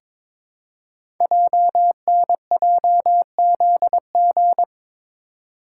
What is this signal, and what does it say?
Morse code, heard as a keyed tone, about 22 wpm, 700 Hz. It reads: JNJZG
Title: Code Groups - 5 Letters - 22wpm